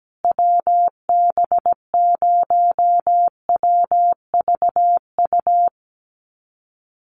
Morse code, heard as a keyed tone, about 17 wpm, 700 Hz. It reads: WB0WVU
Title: USA Call Signs Repeated 2x 17wpm